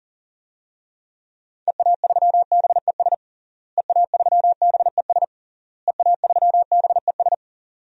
Morse code, heard as a keyed tone, about 42 wpm, 700 Hz. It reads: EA3BES EA3BES EA3BES
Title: DXCC-100 Call Signs - 6 Characters - Repeated 3x - 40wpm